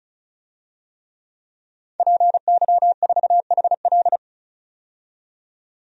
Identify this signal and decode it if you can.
Morse code, heard as a keyed tone, about 36 wpm, 700 Hz. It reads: PY4HL
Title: DXCC-100 Call Signs - 5 Characters - 35wpm